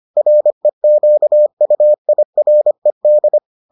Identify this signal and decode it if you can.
Morse code, heard as a keyed tone, about 25 wpm, 600 Hz. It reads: REQUIRED